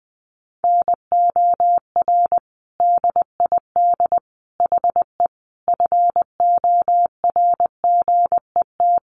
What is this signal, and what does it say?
Morse code, heard as a keyed tone, about 20 wpm, 700 Hz. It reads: NOR DID HE FORGET
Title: Sentences from Top 1500 Words 20wpm